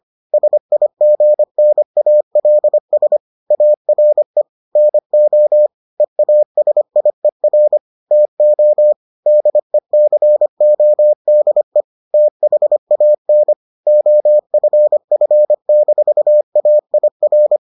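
Morse code, heard as a keyed tone, about 25 wpm, 600 Hz. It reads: SIGNALS ARE NO EASIER TO DECODE THAN OFF-AIR
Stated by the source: Morse edition of the Foundations of Amateur Radio episodes.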